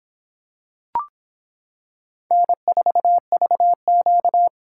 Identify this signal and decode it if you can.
Morse code, heard as a keyed tone, about 25 wpm, 700 Hz. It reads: N4VQ